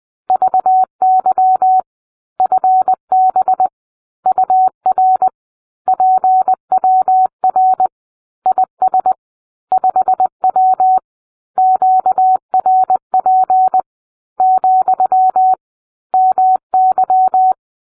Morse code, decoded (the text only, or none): VY FB UR PWR IS 5W QRP , MY